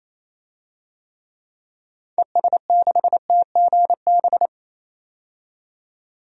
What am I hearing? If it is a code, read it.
Morse code, heard as a keyed tone, about 28 wpm, 700 Hz. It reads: ES6TGB
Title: DXCC-100 Call Signs - 6 Characters - 28wpm